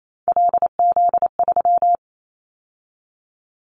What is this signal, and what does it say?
Morse code, heard as a keyed tone, about 28 wpm, 700 Hz. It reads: LZ3
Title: Callsign Prefixes for Top 100 DXCC - 28wpm